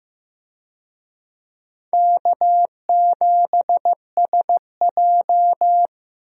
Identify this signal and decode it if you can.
Morse code, heard as a keyed tone, about 16 wpm, 700 Hz. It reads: K7SJ
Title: USA Call Signs - 4 Characters - Repeated 2x - 15wpm